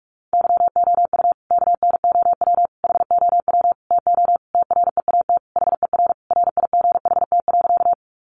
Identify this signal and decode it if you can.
Morse code, heard as a keyed tone, about 44 wpm, 700 Hz. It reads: YOU KNOW HOW TO TREAT HER RIGHT.
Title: Sentences from Top 1300 Words 45wpm